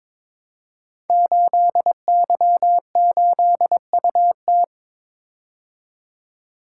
Morse code, heard as a keyed tone, about 23 wpm, 700 Hz. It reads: 8Y8UT
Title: Code Groups - 5 Letter:Numbers - 22wpm